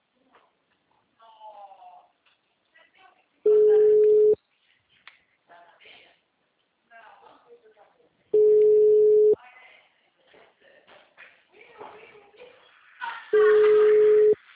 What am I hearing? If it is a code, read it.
Ringback tone (Europe, 425 Hz), with speech.
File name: out- -2014- - - .9339